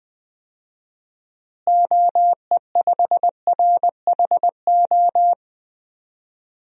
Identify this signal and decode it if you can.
Morse code, heard as a keyed tone, about 21 wpm, 700 Hz. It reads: OE5RHO
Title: DXCC-100 Call Signs - 6 Characters - Repeated 3x - 20wpm